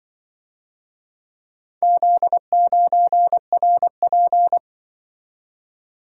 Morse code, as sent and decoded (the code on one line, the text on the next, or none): --.. ----. .-. .--.
Z9RP